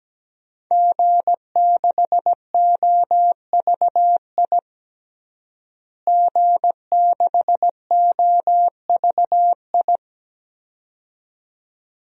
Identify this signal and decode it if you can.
Morse code, heard as a keyed tone, about 17 wpm, 700 Hz. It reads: G6OVI G6OVI
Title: DXCC-100 Call Signs - 5 Characters - Repeated 2x - 17wpm